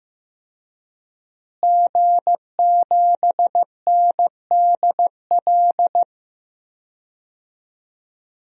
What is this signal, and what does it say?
Morse code, heard as a keyed tone, about 15 wpm, 700 Hz. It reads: G7NDL
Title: DXCC-100 Call Signs - 5 Characters - Repeated 3x - 15wpm